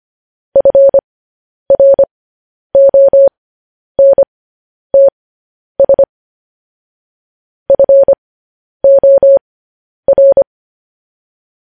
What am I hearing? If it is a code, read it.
Morse code, heard as a keyed tone, about 25 wpm, spaced as for 10 wpm, 550 Hz. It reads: FRONTS FOR
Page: news headlines in morse code at 10 wpm